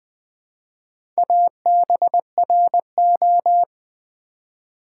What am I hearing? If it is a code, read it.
Morse code, heard as a keyed tone, about 20 wpm, 700 Hz. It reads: ABRO
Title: Build A Word - 2 Letters At A Time - 6-Letter Words - 20wpm